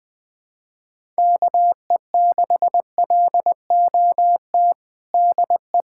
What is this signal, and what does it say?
Morse code, heard as a keyed tone, about 20 wpm, 700 Hz. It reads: KE6LOT DE
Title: USA Callsign DE Callsign 20wpm